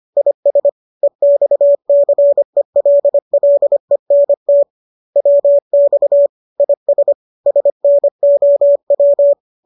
Morse code, heard as a keyed tone, about 25 wpm, 570 Hz. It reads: IS EXCELLENT WX IS SNOW